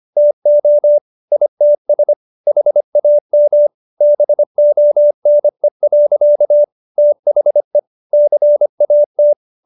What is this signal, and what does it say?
Morse code, heard as a keyed tone, about 25 wpm, 590 Hz. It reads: TO ITS HAM BONE. THE CAT